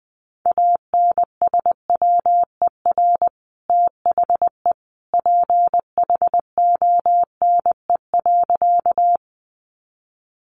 Morse code, heard as a keyed tone, about 20 wpm, 700 Hz. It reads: ANSWER THE PHONE.
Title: Sentences from Top 1100 Words 20wpm